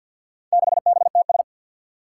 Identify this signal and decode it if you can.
Morse code, heard as a keyed tone, about 47 wpm, 700 Hz. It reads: 6BTS
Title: Four Challenging Letter-Number Combinations 50wpm